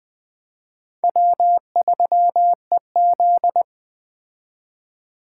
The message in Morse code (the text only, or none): W3EZ